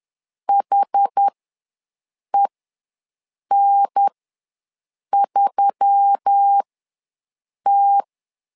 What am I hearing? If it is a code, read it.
Morse code, heard as a keyed tone, about 10 wpm, spaced as for 6 wpm, 800 Hz. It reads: HEN3T
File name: LEZIONI / 4 Q1S3 ws 10wpm 2437